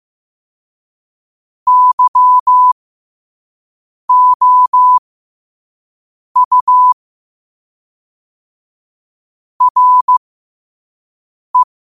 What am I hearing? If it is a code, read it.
Morse code, heard as a keyed tone, about 15 wpm, spaced as for 5 wpm, 1000 Hz. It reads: YOU RE
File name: qotdfiles / 5wpm